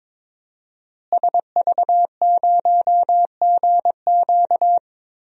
Morse code, heard as a keyed tone, about 22 wpm, 700 Hz. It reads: SV0GQ
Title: DXCC-100 Call Signs - 5 Characters - Repeated 3x - 22wpm